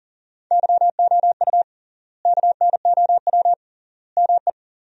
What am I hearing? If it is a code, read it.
Morse code, heard as a keyed tone, about 40 wpm, 700 Hz. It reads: YOU KNOW ME
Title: 3x Word-Spacing - Sentences from Top 400 Words - 40wpm